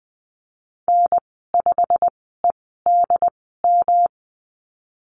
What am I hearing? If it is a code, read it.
Morse code, heard as a keyed tone, about 20 wpm, spaced as for 15 wpm, 700 Hz. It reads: N5EDM
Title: 122 Lesson 20g - Number 3 - Call Signs 20wpm